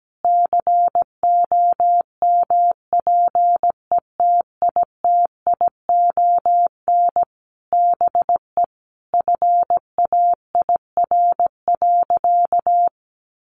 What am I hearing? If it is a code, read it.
Morse code, heard as a keyed tone, about 17 wpm, 700 Hz. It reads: COMPETITION BE FAIR.
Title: Sentences from Top 1300 Words 17wpm